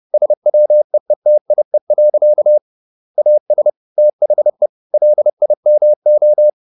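Morse code, heard as a keyed tone, about 30 wpm, 600 Hz. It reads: SWEETIE. AS THE LIMO
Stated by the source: Morse code at 30 wpm, the novel Never by Ken Follett in English, from CW med Gnister.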